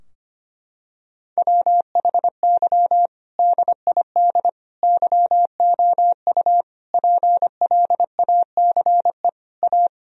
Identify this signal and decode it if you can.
Morse code, heard as a keyed tone, about 25 wpm, 700 Hz. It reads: WHY DID YOU PLACE A
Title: Sentences from Top 1000 Words - Review Part 4 - 25wpm